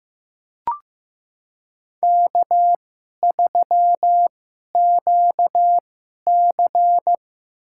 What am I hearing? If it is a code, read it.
Morse code, heard as a keyed tone, about 16 wpm, spaced as for 11 wpm, 700 Hz. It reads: K3QC
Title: USA Call Signs - 4 Characters - 2x Character Spacing - 15wpm